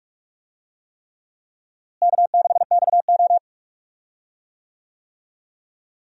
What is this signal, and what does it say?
Morse code, heard as a keyed tone, about 46 wpm, 700 Hz. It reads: K6XO
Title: USA Call Signs - 4 Characters - Repeated 3x - 45wpm